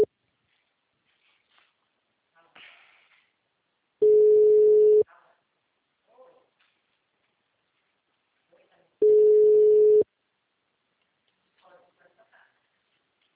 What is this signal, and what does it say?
Ringback tone (Europe, 425 Hz).